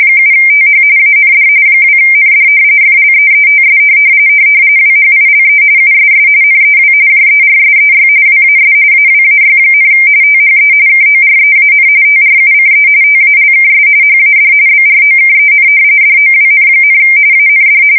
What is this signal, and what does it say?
RTTY, about 45.45 baud. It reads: 7TH AND 2ND FLEETS WHILE USS NEW JERSEY WAS AWAY, THE KOREAN TRUCE TALKS HAD BROKEN DOWN, AND
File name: fsk170 / battleshipnj